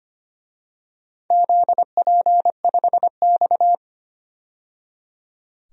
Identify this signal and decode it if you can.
Morse code, heard as a keyed tone, about 25 wpm, 700 Hz. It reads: ZP5X
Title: DXCC-100 Call Signs - 4 Characters - 25wpm